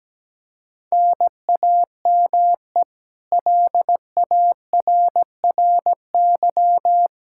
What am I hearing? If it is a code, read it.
Morse code, heard as a keyed tone, about 17 wpm, 700 Hz. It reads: NAME LARRY
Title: Practice QSO - Exchange 1 - Beginning - 17wpm